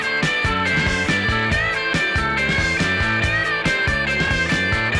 Emergency Alert System (SAME) tones, heard as harsh buzzing bursts. It is heard from 0.6 to 4.5 s.